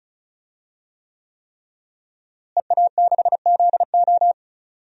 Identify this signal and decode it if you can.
Morse code, heard as a keyed tone, about 36 wpm, 700 Hz. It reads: EA6ZO
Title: DXCC-100 Call Signs - 5 Characters - Repeated 3x - 35wpm